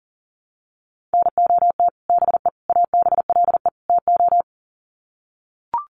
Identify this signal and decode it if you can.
Morse code, heard as a keyed tone, about 40 wpm, 700 Hz. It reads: NOT BE ABLE TO
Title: Sets of 4 Words 40wpm